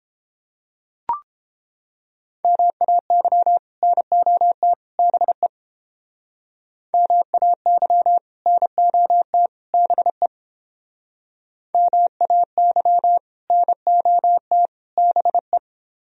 Morse code, heard as a keyed tone, about 29 wpm, 700 Hz. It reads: MAY NOT BE MAY NOT BE MAY NOT BE